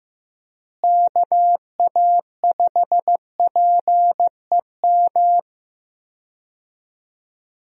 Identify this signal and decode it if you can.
Morse code, heard as a keyed tone, about 15 wpm, 700 Hz. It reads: KA5PEM
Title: USA Call Signs - 6 Characters - Repeated 2x - 15wpm